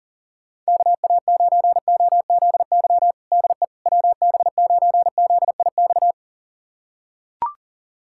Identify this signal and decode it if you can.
Morse code, heard as a keyed tone, about 40 wpm, 700 Hz. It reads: KA9OZY DE WB9ZIX
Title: USA Callsign DE Callsign - 6 Characters - 40wpm